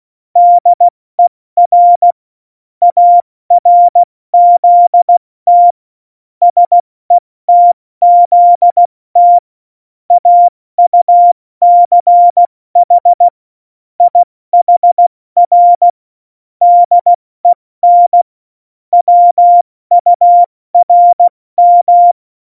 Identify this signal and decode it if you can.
Morse code, heard as a keyed tone, about 16 wpm, spaced as for 14 wpm, 700 Hz. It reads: DER ARZT SETZT AUCH IHR DEN WURM
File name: Morsekurs / Witze / Witz 13 March 2018, 70 F80